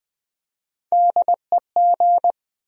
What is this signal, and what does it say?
Morse code, heard as a keyed tone, about 20 wpm, 700 Hz. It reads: DEG
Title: USA Call Sign Postfixes - 3 Letters 20wpm